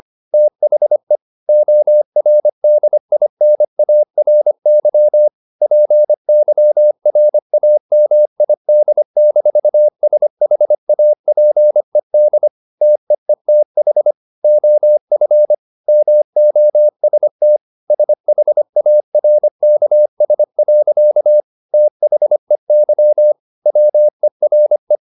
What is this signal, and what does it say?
Morse code, heard as a keyed tone, about 25 wpm, 600 Hz. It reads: THE ORDINARY PYRAMID-SHAPED TEETH OF MOST SHARKS. THEY WERE